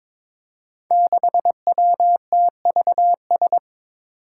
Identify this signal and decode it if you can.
Morse code, heard as a keyed tone, about 22 wpm, 700 Hz. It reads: 6WTVS
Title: Five Challenging Letter-Number Combinations 22wpm